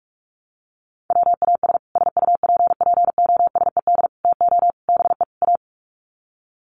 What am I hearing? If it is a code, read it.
Morse code, heard as a keyed tone, about 44 wpm, 700 Hz. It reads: WAS SUPPOSED TO BE A
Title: Sets of 5 Words Encore 45wpm